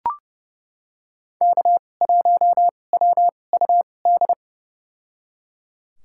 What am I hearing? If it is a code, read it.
Morse code, heard as a keyed tone, about 30 wpm, spaced as for 22 wpm, 700 Hz. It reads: K1WUD
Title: USA Call Signs - 5 Characters - 2x Character Spacing - 30wpm